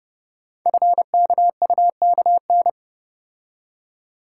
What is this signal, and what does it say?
Morse code, heard as a keyed tone, about 29 wpm, 700 Hz. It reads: FKUKN
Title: Code Groups - 5 Letters - 30wpm